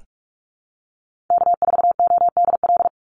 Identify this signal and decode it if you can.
Morse code, heard as a keyed tone, about 44 wpm, 700 Hz. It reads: K4ODL
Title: Call Signs - 45wpm